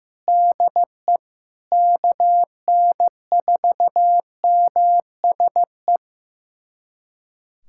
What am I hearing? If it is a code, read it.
Morse code, heard as a keyed tone, about 15 wpm, 700 Hz. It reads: DE KN4MSE
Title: USA Callsign DE Callsign 15wpm